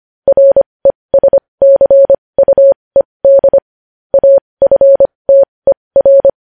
Morse code, heard as a keyed tone, about 25 wpm, spaced as for 20 wpm, 550 Hz. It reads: RESCUED AFTER
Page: news headlines in morse code at 20 wpm